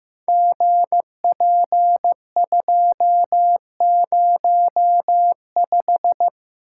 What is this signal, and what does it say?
Morse code, heard as a keyed tone, about 15 wpm, 700 Hz. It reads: GP205
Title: Code Groups - 5 Letter:Numbers - 15wpm